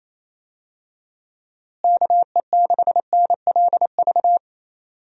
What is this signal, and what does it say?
Morse code, heard as a keyed tone, about 28 wpm, 700 Hz. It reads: KE6NLV
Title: USA Call Signs - 6 Characters - Repeated 3x - 28wpm